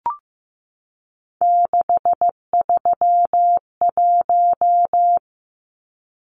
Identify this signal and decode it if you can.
Morse code, heard as a keyed tone, about 15 wpm, 700 Hz. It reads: 631
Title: Numbers - 3-Digit 15wpm